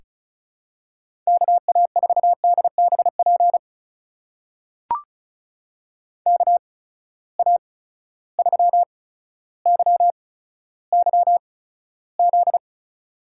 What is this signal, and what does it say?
Morse code, heard as a keyed tone, about 36 wpm, spaced as for 15 wpm, 700 Hz. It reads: KA4DBP KA3YYZ
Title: USA Call Signs - 6 Characters - 8x Character Spacing - 35wpm